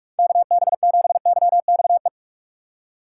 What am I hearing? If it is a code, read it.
Morse code, heard as a keyed tone, about 44 wpm, 700 Hz. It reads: K B 7 Y X E
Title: USA Call Signs - 6 Characters - 6x Character Spacing - 45wpm